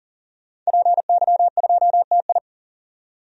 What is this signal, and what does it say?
Morse code, heard as a keyed tone, about 37 wpm, 700 Hz. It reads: PY2TI
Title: DXCC-100 Call Signs - 5 Characters - 40wpm